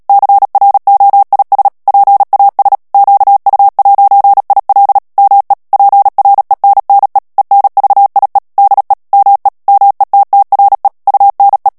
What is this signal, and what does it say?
Morse code, heard as a keyed tone, about 37 wpm, 800 Hz. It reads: CROIS PAS QU'IL ME PRENNE ENVIE DE ME METTRE UNE
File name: CW / FICHIERS-MP3 / FRANCAIS TEXTE CLAIR 2 VITESSE 35 MOTS MINUTES